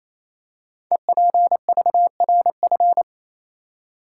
Morse code, heard as a keyed tone, about 27 wpm, 700 Hz. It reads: EPVRF
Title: Code Groups - 5 Letters - 28wpm